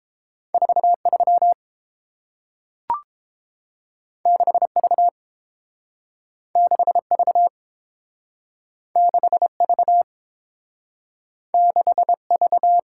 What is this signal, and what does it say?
Morse code, heard as a keyed tone, about 30 wpm, 700 Hz. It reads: 43 6V 6V 6V 6V